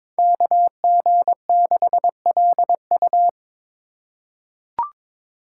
Morse code, heard as a keyed tone, about 21 wpm, 700 Hz. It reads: KG6LU
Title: USA Call Signs - 5 Characters - 4x Character Spacing - 22wpm